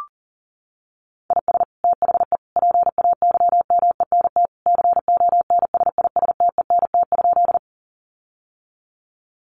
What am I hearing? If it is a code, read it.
Morse code, heard as a keyed tone, about 40 wpm, 700 Hz. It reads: IS THE PAYMENT CONSISTENT?
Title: Sentences from Top 1500 Words 40wpm